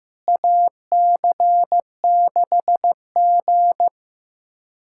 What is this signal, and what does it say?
Morse code, heard as a keyed tone, about 15 wpm, 700 Hz. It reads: AC6G
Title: USA Callsign DE Callsign - 4 Characters - 15wpm